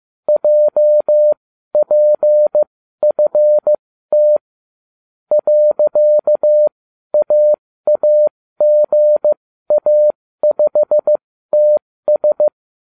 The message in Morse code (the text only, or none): JPFT .AAGA5TS